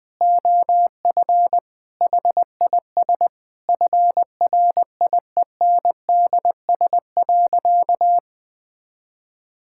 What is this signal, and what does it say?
Morse code, heard as a keyed tone, about 21 wpm, 700 Hz. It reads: OF HIS FRIENDS.
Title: Sentences with 4-Word Phrases - 20wpm